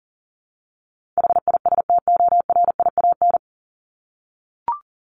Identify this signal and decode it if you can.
Morse code, heard as a keyed tone, about 40 wpm, 700 Hz. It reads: HISTORIAN
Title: Top 1751-2000 Words 40wpm